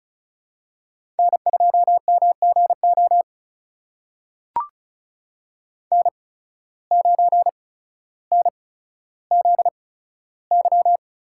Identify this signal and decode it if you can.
Morse code, heard as a keyed tone, about 36 wpm, spaced as for 15 wpm, 700 Hz. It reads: N2MGO N9NZY
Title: USA Call Signs - 8x Character Spacing - 35wpm